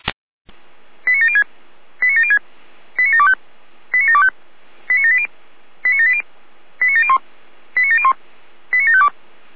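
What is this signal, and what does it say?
Five-tone selective call (ZVEI), heard as quick runs of five tones, tones of about 70 ms. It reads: 89706 89706 89725 89725 89780 89780 89701 89701 89752
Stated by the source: ZVEI five-tone selective call received by SDR.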